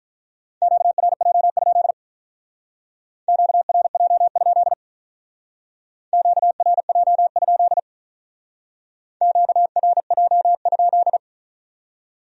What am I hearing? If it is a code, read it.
Morse code, heard as a keyed tone, about 45 wpm, 700 Hz. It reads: QRJ? QRJ? QRJ? QRJ?